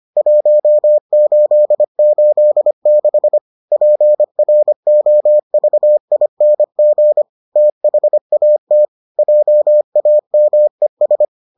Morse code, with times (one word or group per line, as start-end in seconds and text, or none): 0.2-3.4 1886
3.7-7.2 PROVING
7.6-8.8 THAT
9.2-11.2 JAMES